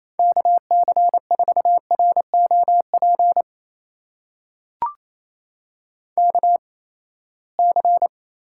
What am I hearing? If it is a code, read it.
Morse code, heard as a keyed tone, about 27 wpm, spaced as for 19 wpm, 700 Hz. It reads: KC4ROP KC
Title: USA Call Signs - 6 Characters - 8x Character Spacing - 28wpm